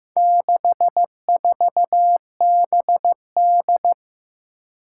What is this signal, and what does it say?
Morse code, heard as a keyed tone, about 15 wpm, 700 Hz. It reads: 64BD
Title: Four Challenging Letter-Number Combinations 15wpm